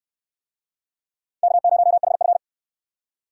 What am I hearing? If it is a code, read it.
Morse code, heard as a keyed tone, about 68 wpm, 700 Hz. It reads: K0HF